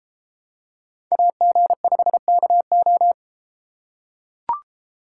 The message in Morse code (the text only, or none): AG5KO